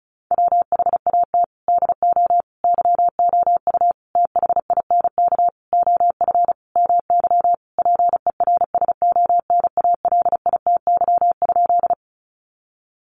Morse code, heard as a keyed tone, about 34 wpm, 700 Hz. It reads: WHAT DO YOU THINK OF MY PERSONALITY?
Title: Sentences from Top 1400 Words 35wpm